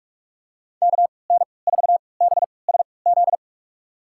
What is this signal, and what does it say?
Morse code, heard as a keyed tone, about 47 wpm, spaced as for 26 wpm, 700 Hz. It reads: KN4BSZ
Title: USA Call Signs - 6 Characters - 3x Character Spacing - 45wpm